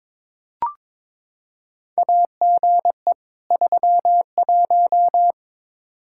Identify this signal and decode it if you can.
Morse code, heard as a keyed tone, about 22 wpm, 700 Hz. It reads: AGE 31
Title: K1USN SST QSO Practice 22wpm